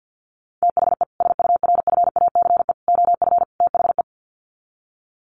Morse code, heard as a keyed tone, about 50 wpm, 700 Hz. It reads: THE SURFACE OF THE
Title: Sets of 4 Words 50wpm